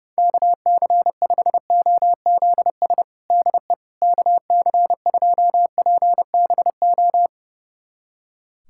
Morse code, heard as a keyed tone, about 30 wpm, 700 Hz. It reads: KC5OZS DE KC2PBO
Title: USA Callsign DE Callsign 30wpm